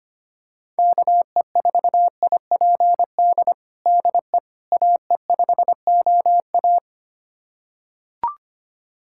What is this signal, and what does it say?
Morse code, heard as a keyed tone, about 25 wpm, 700 Hz. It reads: KE4IPD DE AE5OA
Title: USA Callsign DE Callsign 25wpm